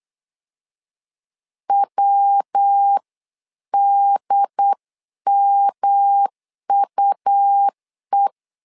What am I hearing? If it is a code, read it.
Morse code, heard as a keyed tone, about 8 wpm, 800 Hz. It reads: WDMUE
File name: LEZIONI / 4 Q1S 8wpm 2710